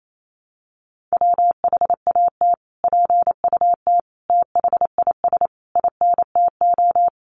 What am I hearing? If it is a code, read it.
Morse code, heard as a keyed tone, about 28 wpm, 700 Hz. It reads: WHAT PUT THIS INTO
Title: Sentences from Top 600 Words 28wpm